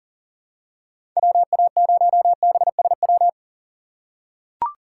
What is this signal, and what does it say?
Morse code, heard as a keyed tone, about 37 wpm, 700 Hz. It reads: WA0BSW
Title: USA Call Signs - 6 Characters - 2x Character Spacing - 40wpm